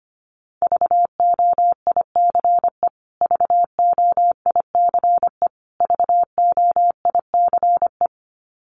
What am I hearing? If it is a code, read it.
Morse code, heard as a keyed tone, about 25 wpm, 700 Hz. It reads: VOICE VOICE VOICE
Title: Top 500 Words Repeated 25wpm